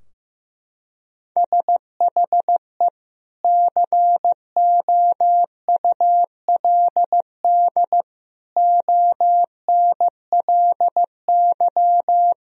Morse code, heard as a keyed tone, about 16 wpm, 700 Hz. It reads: SHE COULD ONLY
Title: Sentences from Top 1000 Words - Review Part 2 - 15wpm